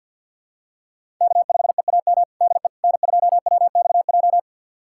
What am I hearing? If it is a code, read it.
Morse code, heard as a keyed tone, about 51 wpm, 700 Hz. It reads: K5EAM DE N2WXJ
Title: USA Callsign DE Callsign - 5 Characters - 50wpm